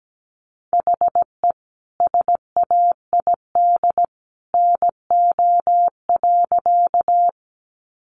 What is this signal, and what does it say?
Morse code, heard as a keyed tone, about 17 wpm, 700 Hz. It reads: HE SAID NO.
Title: Sentences from Top 100 Words 17wpm